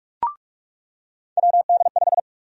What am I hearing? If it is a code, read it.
Morse code, heard as a keyed tone, about 46 wpm, 700 Hz. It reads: WD5
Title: USA Call Sign Prefixes - 45wpm